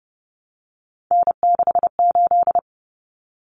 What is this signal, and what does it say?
Morse code, heard as a keyed tone, about 30 wpm, 700 Hz. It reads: N68
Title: Common Cut Numbers - 3-Digit 30wpm